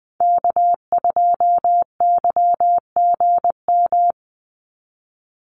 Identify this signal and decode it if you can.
Morse code, heard as a keyed tone, about 20 wpm, 700 Hz. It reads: K2YGM
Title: Call Signs Repeated 2x - 20wpm